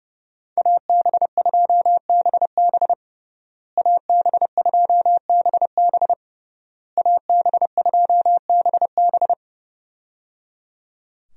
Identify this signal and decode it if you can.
Morse code, heard as a keyed tone, about 30 wpm, 700 Hz. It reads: AB2BB AB2BB AB2BB
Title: USA Call Signs - 5 Characters - Repeated 3x - 30wpm